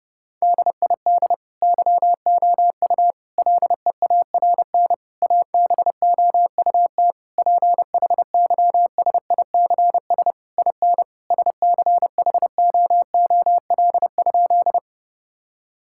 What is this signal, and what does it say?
Morse code, heard as a keyed tone, about 30 wpm, 700 Hz. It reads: DID YOU LEARN ABOUT PHYSICS IN SCHOOL?
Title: Sentences from Top 1000 Words - Review Part 5 - 30wpm